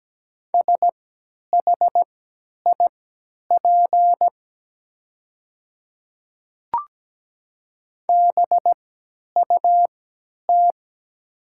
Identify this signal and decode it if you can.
Morse code, heard as a keyed tone, about 17 wpm, spaced as for 10 wpm, 700 Hz. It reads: SHIP BUT
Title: 3x Character Spacing - Top 500 Words - 17wpm